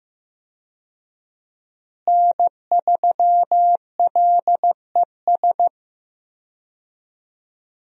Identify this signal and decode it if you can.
Morse code, heard as a keyed tone, about 15 wpm, 700 Hz. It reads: N3LES
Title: USA Call Signs - 5 Characters - Repeated 2x - 15wpm